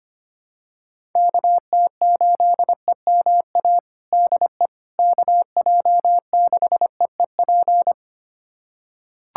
Morse code, heard as a keyed tone, about 25 wpm, 700 Hz. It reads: KT8EMA DE KJ6EEP
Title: USA Callsign DE Callsign 25wpm